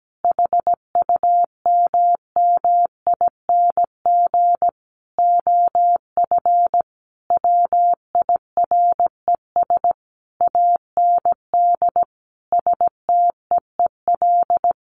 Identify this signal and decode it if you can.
Morse code, heard as a keyed tone, about 17 wpm, 700 Hz. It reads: HUMMING OF WIRES AND STEEL
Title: A Sound Of Thunder 17wpm